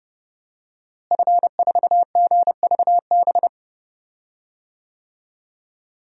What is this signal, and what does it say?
Morse code, heard as a keyed tone, about 29 wpm, 700 Hz. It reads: F4GVB
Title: DXCC-100 Call Signs - 5 Characters - Repeated 3x - 30wpm